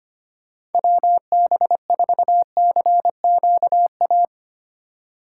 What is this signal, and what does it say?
Morse code, heard as a keyed tone, about 33 wpm, 700 Hz. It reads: WB4CQA